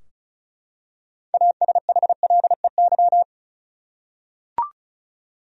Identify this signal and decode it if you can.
Morse code, heard as a keyed tone, about 36 wpm, 700 Hz. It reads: ASHLEY
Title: USA Common Names - 35wpm